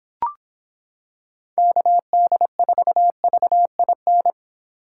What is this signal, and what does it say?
Morse code, heard as a keyed tone, about 26 wpm, 700 Hz. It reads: KD4VIN